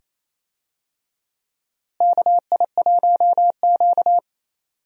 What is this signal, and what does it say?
Morse code, heard as a keyed tone, about 28 wpm, 700 Hz. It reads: KI1Q
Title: USA Call Signs - 4 Characters - Repeated 2x - 28wpm